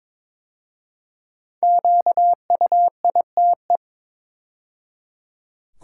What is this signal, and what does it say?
Morse code, heard as a keyed tone, about 22 wpm, 700 Hz. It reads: QUITE
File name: 5 Letter Words - First 3 Letters Given - 22wpm